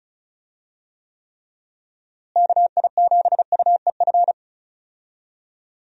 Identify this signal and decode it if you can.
Morse code, heard as a keyed tone, about 34 wpm, 700 Hz. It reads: KI7UEF
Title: USA Call Signs - 6 Characters - Repeated 3x - 35wpm